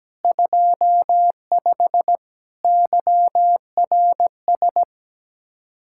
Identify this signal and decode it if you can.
Morse code, heard as a keyed tone, about 17 wpm, 700 Hz. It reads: 25 YRS
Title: Speed-Racing Common QSO Phrases - 17wpm